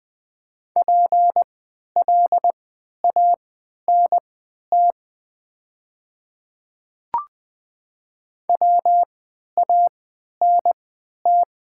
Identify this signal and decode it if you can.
Morse code, heard as a keyed tone, about 20 wpm, spaced as for 11 wpm, 700 Hz. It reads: PLANT WANT
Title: 3x Character Spacing - Top 500 Words - 20wpm